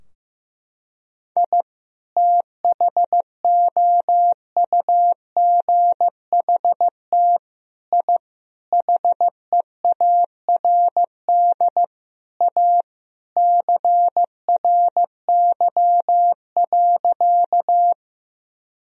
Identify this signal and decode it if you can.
Morse code, heard as a keyed tone, about 15 wpm, 700 Hz. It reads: I THOUGHT I HEARD A CRY.
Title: Sentences from Top 1000 Words - Review Part 2 - 15wpm